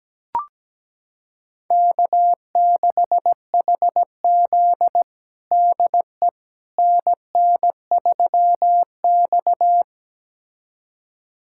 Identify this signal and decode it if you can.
Morse code, heard as a keyed tone, about 17 wpm, 700 Hz. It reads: K6HZ DE NN3X
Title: USA Callsign DE Callsign - 4 Characters - 17wpm